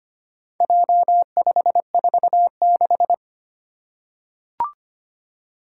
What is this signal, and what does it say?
Morse code, heard as a keyed tone, about 25 wpm, 700 Hz. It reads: J546
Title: Four Challenging Letter-Number Combinations 25wpm